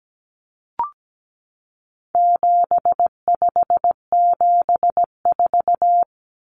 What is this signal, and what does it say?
Morse code, heard as a keyed tone, about 17 wpm, 700 Hz. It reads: 7574
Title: Common Cut Numbers - 4-Digit 17wpm